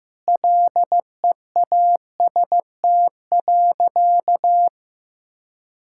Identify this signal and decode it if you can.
Morse code, heard as a keyed tone, about 15 wpm, 700 Hz. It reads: LEAST.
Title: Sentences from Top 1000 Words - Review Part 5 - 15wpm